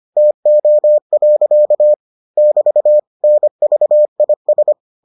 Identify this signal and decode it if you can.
Morse code, heard as a keyed tone, about 25 wpm, 600 Hz. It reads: TO. <BT> NVIS